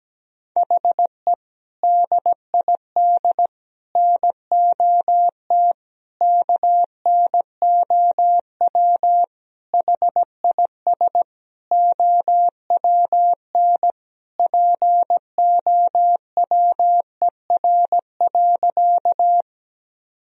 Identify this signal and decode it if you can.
Morse code, heard as a keyed tone, about 17 wpm, 700 Hz. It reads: HE DID NOT KNOW HIS OWN POWER.
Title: Sentences from Top 1000 Words - Review Part 2 - 17wpm